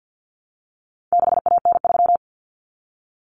Morse code, heard as a keyed tone, about 50 wpm, 700 Hz. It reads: 6AN3
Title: Common Cut Numbers - 4-Digit 50wpm